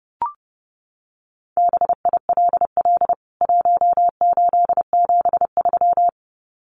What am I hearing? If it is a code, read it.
Morse code, heard as a keyed tone, about 30 wpm, 700 Hz. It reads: BILL 1873
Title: CWT Exchange 30wpm